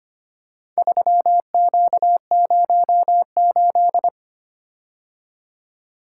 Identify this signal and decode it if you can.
Morse code, heard as a keyed tone, about 25 wpm, 700 Hz. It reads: 3Q08
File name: practice / 25wpm / Four Letter-Number Combinations - 25wpm